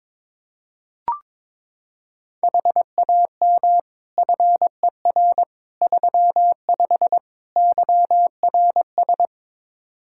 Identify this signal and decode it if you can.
Morse code, heard as a keyed tone, about 21 wpm, 700 Hz. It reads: HAM FER 35 YRS
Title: K1USN SST QSO Practice 22wpm